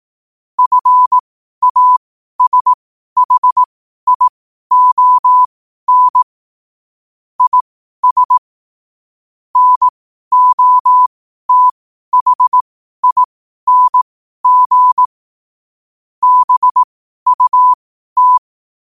Morse code, heard as a keyed tone, about 18 wpm, spaced as for 12 wpm, 1000 Hz. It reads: FASHION IS NOTHING BUT
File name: qotdfiles / 12wpm